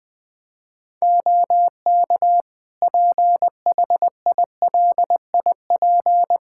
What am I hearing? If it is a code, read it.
Morse code, heard as a keyed tone, about 20 wpm, 700 Hz. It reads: OK PHILIP
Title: Practice QSO - Exchange 4 - Ending - 20wpm